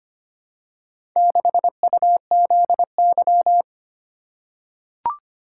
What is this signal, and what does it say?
Morse code, heard as a keyed tone, about 25 wpm, 700 Hz. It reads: 6UZY
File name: practice / 25wpm / Four Letter-Number Combinations - 25wpm